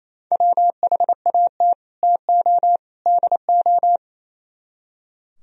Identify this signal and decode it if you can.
Morse code, heard as a keyed tone, about 34 wpm, 700 Hz. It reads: WHAT TO DO
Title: Speed-Racing - Sets of 3 Words 28wpm